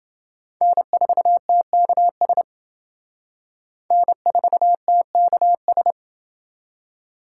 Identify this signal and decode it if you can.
Morse code, heard as a keyed tone, about 28 wpm, 700 Hz. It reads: N4TKS N4TKS